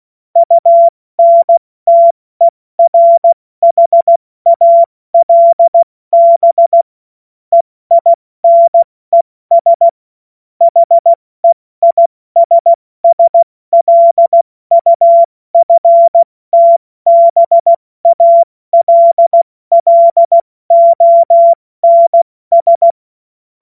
Morse code, heard as a keyed tone, about 16 wpm, 680 Hz. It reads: UNTERHALB EINES HEISSLUFTBALLONS
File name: Morsekurs / Witze / Witz 03. Mai 2022 70 F80